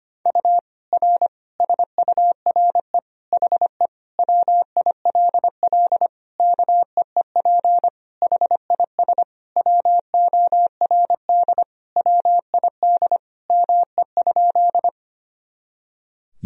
Morse code, heard as a keyed tone, about 25 wpm, 700 Hz. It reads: U R SURE HE WILL KEEP HIS WORD WID ME?
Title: Sentences from Top 300 Words and Abbreviations 25wpm